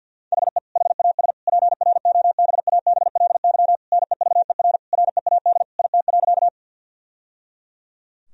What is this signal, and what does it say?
Morse code, heard as a keyed tone, about 50 wpm, 700 Hz. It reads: HE HAS PROBABLY NEVER READ IT.
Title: Sentences from Top 1000 Words - Review Part 4 - 50wpm